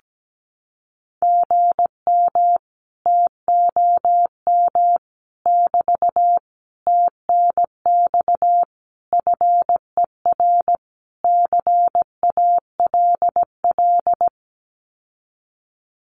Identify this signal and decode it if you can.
Morse code, heard as a keyed tone, about 17 wpm, 700 Hz. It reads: GM TOM <BT> TNX FER CALL
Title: Common QSO Phrases Encore 17wpm